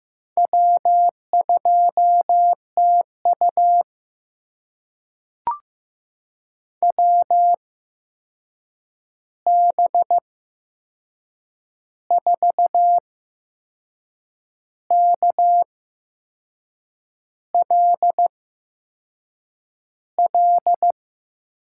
Morse code, heard as a keyed tone, about 15 wpm, spaced as for 6 wpm, 700 Hz. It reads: W2TU WB4KLL
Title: USA Call Signs - 8x Character Spacing - 15wpm